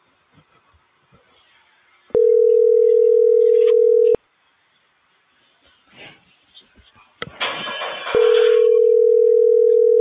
Ringback tone (North America), 440 + 480 Hz.